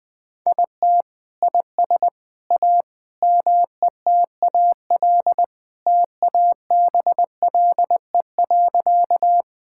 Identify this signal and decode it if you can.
Morse code, heard as a keyed tone, about 21 wpm, 700 Hz. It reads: IT IS A METAL TABLE.
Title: Sentences from Top 1000 Words - Review Part 6 - 20wpm